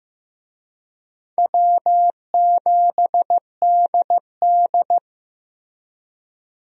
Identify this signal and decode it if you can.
Morse code, heard as a keyed tone, about 15 wpm, 700 Hz. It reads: W7DD
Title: USA Call Signs - 4 Characters - Repeated 3x - 15wpm